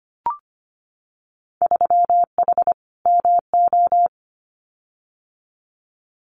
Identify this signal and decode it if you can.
Morse code, heard as a keyed tone, about 25 wpm, 700 Hz. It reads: 3H MO
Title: Winter Field Day Exchanges Encore 25wpm